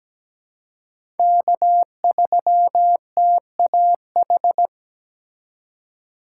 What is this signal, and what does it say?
Morse code, heard as a keyed tone, about 17 wpm, 700 Hz. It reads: K3TAH
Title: USA Call Signs - 5 Characters - 4x Character Spacing - 17wpm